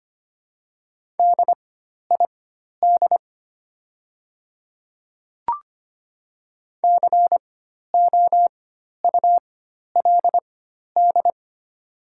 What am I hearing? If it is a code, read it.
Morse code, heard as a keyed tone, about 25 wpm, spaced as for 12 wpm, 700 Hz. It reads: DID COULD
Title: 4x Character Spacing - Top 200 Words - 25wpm